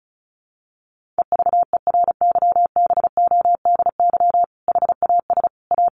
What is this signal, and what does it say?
Morse code, heard as a keyed tone, about 34 wpm, 700 Hz. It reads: EVERYBODY HAS A
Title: Sentences from Top 1400 Words 35wpm